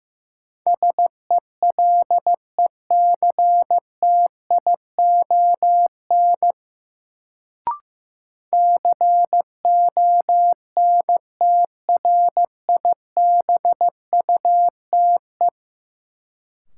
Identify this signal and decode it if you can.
Morse code, heard as a keyed tone, about 15 wpm, 700 Hz. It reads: SELECTION CONTRIBUTE
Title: Sentences from Top 1400 Words - New Word Drill - 15wpm